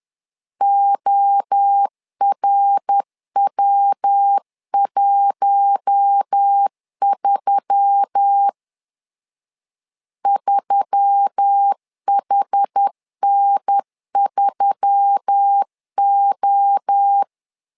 Morse code, heard as a keyed tone, about 10 wpm, 800 Hz. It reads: ORW13 3HN3O
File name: LEZIONI / 4 Q1S3 10wpm 2427